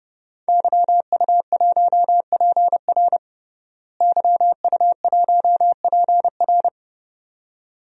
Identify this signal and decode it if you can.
Morse code, heard as a keyed tone, about 32 wpm, 700 Hz. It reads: YU1PR YU1PR
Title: DXCC-100 Call Signs - 5 Characters - Repeated 2x - 30wpm